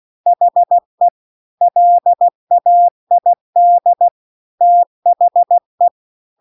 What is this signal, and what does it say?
Morse code, heard as a keyed tone, about 16 wpm, 700 Hz. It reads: HE LAID THE